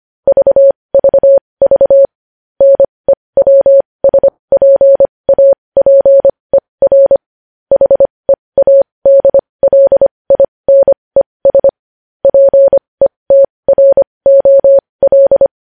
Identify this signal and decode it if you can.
Morse code, heard as a keyed tone, about 25 wpm, spaced as for 20 wpm, 550 Hz. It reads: VVV NEWSPAPER HEADLINES PETROL
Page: news headlines in morse code at 20 wpm